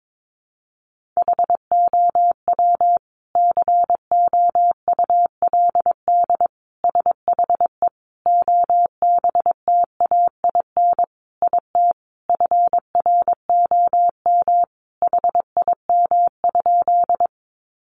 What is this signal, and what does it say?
Morse code, heard as a keyed tone, about 22 wpm, 700 Hz. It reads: HOW COULD SHE OBTAIN IT FROM HIM?
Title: Sentences from Top 1300 Words 22wpm